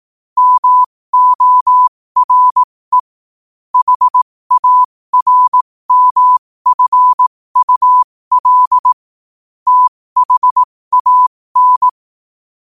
Morse code, heard as a keyed tone, about 18 wpm, spaced as for 15 wpm, 1000 Hz. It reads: MORE HARMFUL THAN
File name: qotdfiles / 15wpm